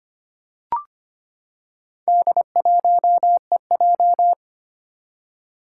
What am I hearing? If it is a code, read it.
Morse code, heard as a keyed tone, about 25 wpm, 700 Hz. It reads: D1EJ
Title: Four Challenging Letter-Number Combinations 25wpm